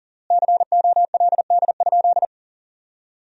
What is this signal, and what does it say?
Morse code, heard as a keyed tone, about 40 wpm, 700 Hz. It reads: COLD?
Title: 4x Word-Spacing - Sentences from Top 400 Words - 40wpm